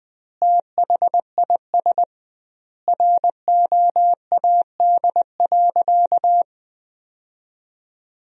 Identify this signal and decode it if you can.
Morse code, heard as a keyed tone, about 20 wpm, 700 Hz. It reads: THIS ROAD.
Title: 2x Word-Spacing - Sentences from Top 400 Words - 20wpm